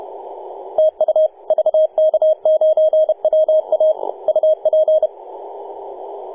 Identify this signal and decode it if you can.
Morse code, heard as a keyed tone, about 30 wpm, 630 Hz. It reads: TU VK9WA UP
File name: posts / 2015-nov-18-0552z-vk9wa-24-894mhz-cw